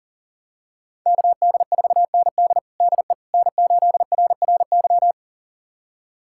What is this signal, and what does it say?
Morse code, heard as a keyed tone, about 37 wpm, 700 Hz. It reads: KD4ND DE N8RRY
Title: USA Callsign DE Callsign - 5 Characters - 40wpm